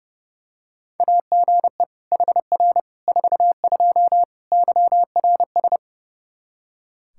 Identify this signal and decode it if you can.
Morse code, heard as a keyed tone, about 30 wpm, 700 Hz. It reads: AGE HR 42 YRS
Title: Practice QSO - Exchange 3 - More Info - 30wpm